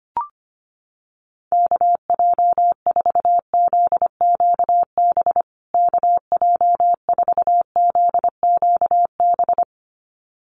Morse code, heard as a keyed tone, about 25 wpm, 700 Hz. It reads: KJ4ZQB KJ4ZQB
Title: USA Call Signs Repeated 2x 25wpm